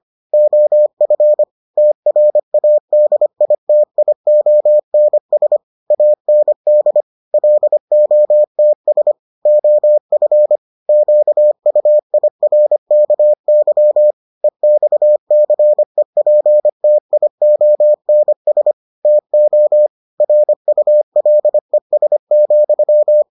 Morse code, heard as a keyed tone, about 25 wpm, 600 Hz. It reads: OF TRADITIONS AND LOTS OF QUIRKY EXCEPTIONS TO RULES,